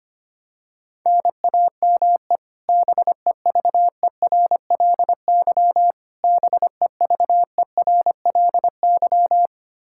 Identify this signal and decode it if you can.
Morse code, heard as a keyed tone, about 25 wpm, 700 Hz. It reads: NAME BEVERLY BEVERLY
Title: Practice QSO - Exchange 1 - Beginning - 25wpm